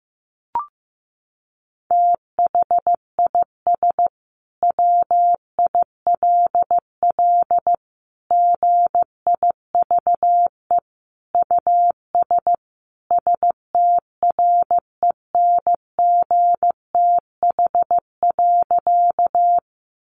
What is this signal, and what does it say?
Morse code, heard as a keyed tone, about 14 wpm, 700 Hz. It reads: THIS WILL GIVE US STRENGTH.
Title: Sentences from Top 1300 Words 15wpm